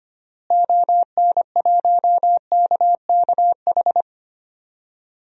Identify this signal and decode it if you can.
Morse code, heard as a keyed tone, about 25 wpm, 700 Hz. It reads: ON1KKH
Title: DXCC-100 Call Signs - 6 Characters - Repeated 2x - 25wpm